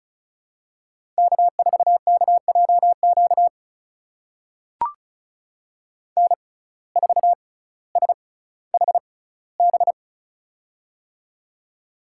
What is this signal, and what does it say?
Morse code, heard as a keyed tone, about 36 wpm, spaced as for 18 wpm, 700 Hz. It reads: K4KJQ N4SHB
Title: USA Call Signs - 5 Characters - 6x Character Spacing - 35wpm